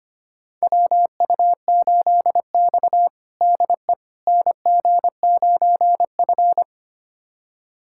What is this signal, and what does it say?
Morse code, heard as a keyed tone, about 25 wpm, 700 Hz. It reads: WU8X DE NG9F
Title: USA Callsign DE Callsign - 4 Characters - 25wpm